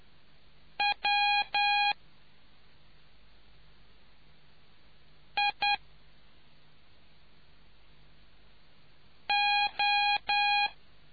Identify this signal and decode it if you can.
Morse code, heard as a keyed tone, about 10 wpm, 820 Hz. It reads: W I O